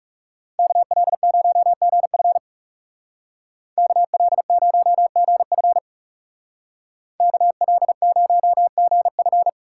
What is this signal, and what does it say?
Morse code, heard as a keyed tone, about 36 wpm, 700 Hz. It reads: KL0GF KL0GF KL0GF
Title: Speed Racing - USA 2x2 Call Signs 30wpm